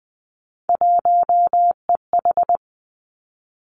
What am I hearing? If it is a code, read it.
Morse code, heard as a keyed tone, about 20 wpm, 700 Hz. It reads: 1EH
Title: Three Challenging Letter-Number Combinations 20wpm